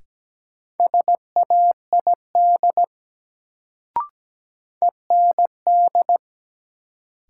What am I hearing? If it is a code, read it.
Morse code, heard as a keyed tone, about 17 wpm, 700 Hz. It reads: SAID END
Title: Sets of 2 Words - New Word Drill - 17wpm